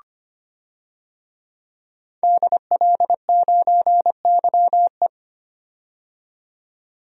Morse code, heard as a keyed tone, about 25 wpm, 700 Hz. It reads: DL9YE
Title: DXCC-100 Call Signs - 5 Characters - 25wpm